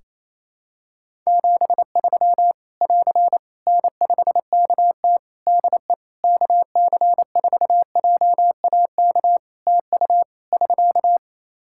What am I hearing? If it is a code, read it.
Morse code, heard as a keyed tone, about 28 wpm, 700 Hz. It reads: 73 <AR> N5KT DE KC4JAK TU <SK>
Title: K1USN SST QSO Practice 28wpm